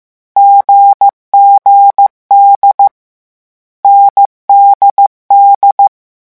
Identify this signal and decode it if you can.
Morse code, heard as a keyed tone, about 15 wpm, 800 Hz. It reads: GGD NDD
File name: morse mp3 / lecon4-vitesse 14